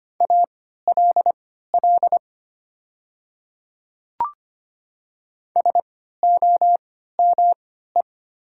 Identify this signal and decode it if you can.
Morse code, heard as a keyed tone, about 25 wpm, spaced as for 14 wpm, 700 Hz. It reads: ALL SOME
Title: 3x Character Spacing - Top 500 Words - 25wpm